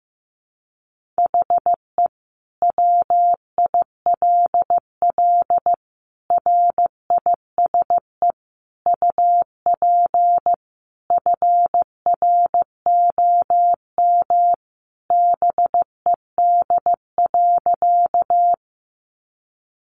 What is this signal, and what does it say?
Morse code, heard as a keyed tone, about 15 wpm, 700 Hz. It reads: HE WILL RISE UP FROM BED.
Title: Sentences from Top 900 Words 15wpm